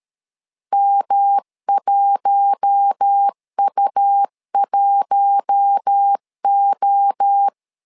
Morse code, heard as a keyed tone, about 13 wpm, 800 Hz. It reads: M1U1O
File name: LEZIONI / 4 Q1 12wpm 2028